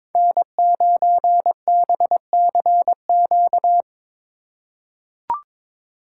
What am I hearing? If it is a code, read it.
Morse code, heard as a keyed tone, about 22 wpm, 700 Hz. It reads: N9BCQ
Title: USA Call Signs - 5 Characters - 3x Character Spacing - 22wpm